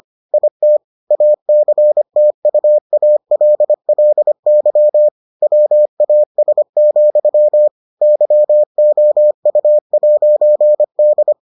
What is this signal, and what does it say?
Morse code, heard as a keyed tone, about 25 wpm, 600 Hz. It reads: IT ACTUALLY WAS, YOU'D